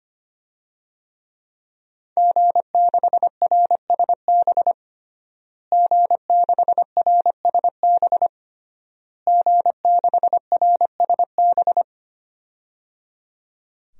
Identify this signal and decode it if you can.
Morse code, heard as a keyed tone, about 25 wpm, 700 Hz. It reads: G6RSB G6RSB G6RSB
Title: DXCC-100 Call Signs - 5 Characters - Repeated 3x - 25wpm